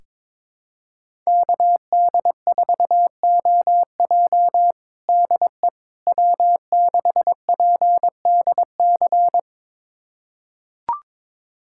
Morse code, heard as a keyed tone, about 22 wpm, 700 Hz. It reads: KD4OJ DE W6PDC
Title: USA Callsign DE Callsign - 5 Characters - 22wpm